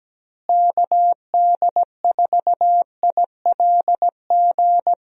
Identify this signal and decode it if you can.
Morse code, heard as a keyed tone, about 17 wpm, 700 Hz. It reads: KD4ILG
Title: USA Call Signs - 6 Characters - Repeated 2x - 17wpm